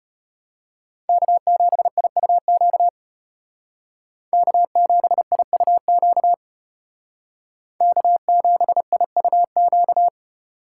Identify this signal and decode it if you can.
Morse code, heard as a keyed tone, about 30 wpm, 700 Hz. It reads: K7IUQ K7IUQ K7IUQ
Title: Speed Racing - USA 1x3 Call Signs 25wpm